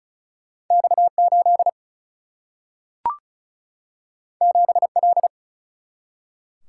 Morse code, heard as a keyed tone, about 36 wpm, 700 Hz. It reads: X 8 7 L
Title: Single-Letter Single-Number 35wpm